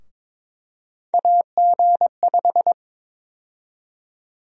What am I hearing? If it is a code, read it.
Morse code, heard as a keyed tone, about 22 wpm, 700 Hz. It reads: AG5
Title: USA Call Sign Prefixes - Repeated 2x - 22wpm